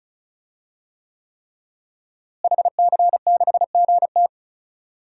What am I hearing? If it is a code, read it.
Morse code, heard as a keyed tone, about 34 wpm, 700 Hz. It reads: HC6GT
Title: DXCC-100 Call Signs - 5 Characters - Repeated 3x - 35wpm